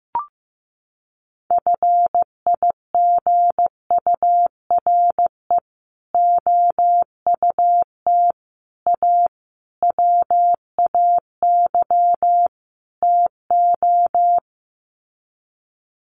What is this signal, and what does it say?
Morse code, heard as a keyed tone, about 15 wpm, 700 Hz. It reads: FIGURE OUT A WAY TO
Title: Sets of 5 Words Encore2 15wpm